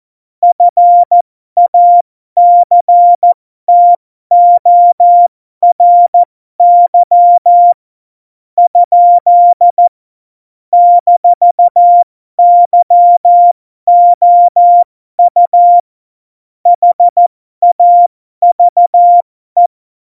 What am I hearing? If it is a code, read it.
Morse code, heard as a keyed tone, about 14 wpm, spaced as for 11 wpm, 700 Hz. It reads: FACTORY? -YOU HAVE
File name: Morsekurs / Witze